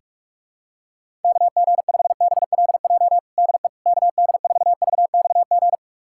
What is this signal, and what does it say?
Morse code, heard as a keyed tone, about 48 wpm, 700 Hz. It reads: KG5BLJ DE KD4VXG
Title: USA Callsign DE Callsign 45wpm